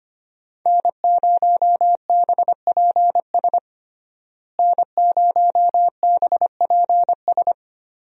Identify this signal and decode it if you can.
Morse code, heard as a keyed tone, about 25 wpm, 700 Hz. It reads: N0BPS N0BPS
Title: USA Call Signs - 5 Characters - Repeated 2x - 25wpm